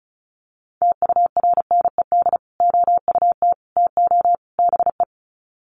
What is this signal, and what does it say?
Morse code, heard as a keyed tone, about 34 wpm, 700 Hz. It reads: TURNED OUT TO BE
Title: Sets of 4 Words 35wpm